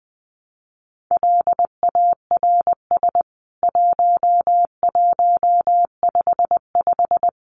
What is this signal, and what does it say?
Morse code, heard as a keyed tone, about 20 wpm, 700 Hz. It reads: LARS 1155
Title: CWT Exchange 20wpm